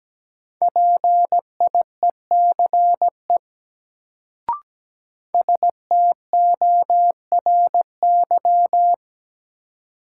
Morse code, heard as a keyed tone, about 17 wpm, 700 Hz. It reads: PIECE STORY
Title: Sets of 3 Words - New Word Drill - 17wpm